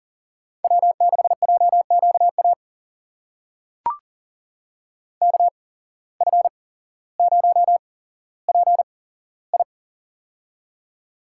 Morse code, heard as a keyed tone, about 40 wpm, spaced as for 17 wpm, 700 Hz. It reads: W6JQA KF0PI
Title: USA Call Signs - 8x Character Spacing - 40wpm